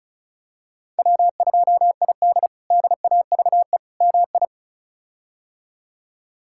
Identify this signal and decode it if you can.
Morse code, heard as a keyed tone, about 36 wpm, 700 Hz. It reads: W2ID DAVE MI
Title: K1USN SST QSO Practice - 35wpm